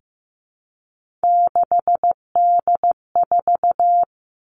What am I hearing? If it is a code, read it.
Morse code, heard as a keyed tone, about 15 wpm, 700 Hz. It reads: 6D4
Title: Three Challenging Letter-Number Combinations 15wpm